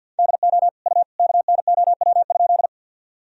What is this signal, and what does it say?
Morse code, heard as a keyed tone, about 52 wpm, 700 Hz. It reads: DO U KNOW?
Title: Sentences from Top 100 Words and Abbreviations 50wpm